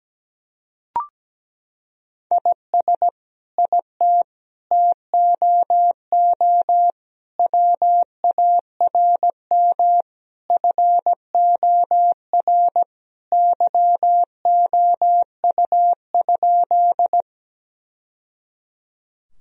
Morse code, heard as a keyed tone, about 17 wpm, 700 Hz. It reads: IS IT TOO WARM FOR YOU?
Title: Sentences from Top 1000 Words - Review Part 2 - 17wpm